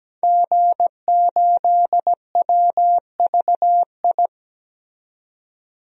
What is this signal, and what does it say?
Morse code, heard as a keyed tone, about 17 wpm, 700 Hz. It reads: G8WVI
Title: DXCC-100 Call Signs - 5 Characters - 17wpm